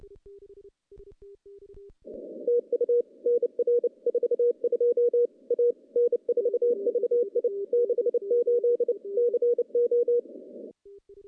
Morse code, heard as a keyed tone, about 29 wpm, 480 Hz. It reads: TU NR 42 A N4VI 68 CO
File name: contests / CW-SS-k5tr- -0100